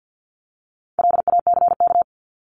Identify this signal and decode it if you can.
Morse code, heard as a keyed tone, about 50 wpm, 700 Hz. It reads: LACK
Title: Top 1001-1250 Words 50wpm